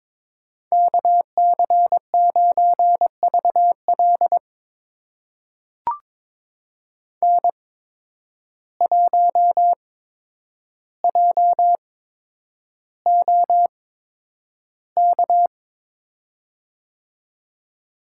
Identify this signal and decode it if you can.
Morse code, heard as a keyed tone, about 22 wpm, spaced as for 9 wpm, 700 Hz. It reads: KC9VL N1JOK
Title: USA Call Signs - 5 Characters - 8x Character Spacing - 22wpm